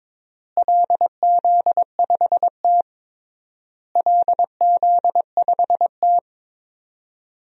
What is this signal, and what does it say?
Morse code, heard as a keyed tone, about 21 wpm, 700 Hz. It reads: LZ5T LZ5T
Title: DXCC-100 Call Signs - 4 Characters - Repeated 3x - 22wpm